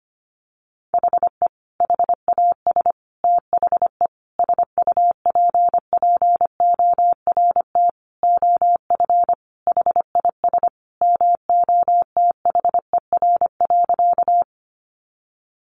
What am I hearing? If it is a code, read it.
Morse code, heard as a keyed tone, about 25 wpm, 700 Hz. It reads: HE HAS THE SUPPORT OF HIS MOTHER.
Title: Sentences from Top 800 Words 25wpm